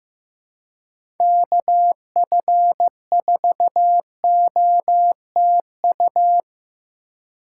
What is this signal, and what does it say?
Morse code, heard as a keyed tone, about 15 wpm, 700 Hz. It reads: KF4OTU
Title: USA Call Signs - 6 Characters - 3x Character Spacing - 15wpm